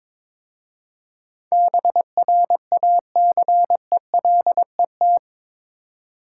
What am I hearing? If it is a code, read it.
Morse code, heard as a keyed tone, about 25 wpm, 700 Hz. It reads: BRACELET
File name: Easy Words - Speed-Racing with Hints - 22wpm